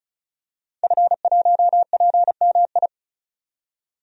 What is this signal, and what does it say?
Morse code, heard as a keyed tone, about 34 wpm, 700 Hz. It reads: F1PMI
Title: DXCC-100 Call Signs - 5 Characters - 35wpm